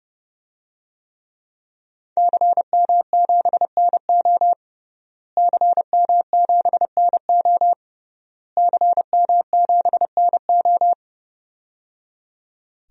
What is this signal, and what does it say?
Morse code, heard as a keyed tone, about 30 wpm, 700 Hz. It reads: CM7NO CM7NO CM7NO
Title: DXCC-100 Call Signs - 5 Characters - Repeated 3x - 30wpm